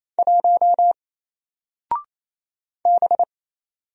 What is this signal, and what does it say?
Morse code, heard as a keyed tone, about 27 wpm, 700 Hz. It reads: 1 B
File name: Single-Letter-Number-28wpm